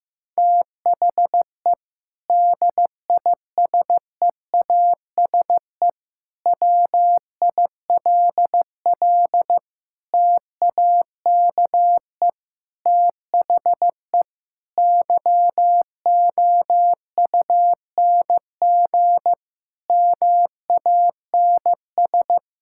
Morse code, heard as a keyed tone, about 15 wpm, 700 Hz. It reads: THE DISEASE WILL TAKE THE YOUNG MANS
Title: Sentences from Top 1000 Words - Review Part 6 - 15wpm